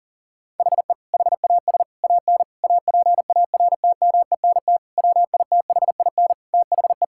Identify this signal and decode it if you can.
Morse code, heard as a keyed tone, about 40 wpm, 700 Hz. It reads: HE HAS AN APARTMENT WITHIN THE
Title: Sentences from Top 1000 Words - Review Part 6 - 40wpm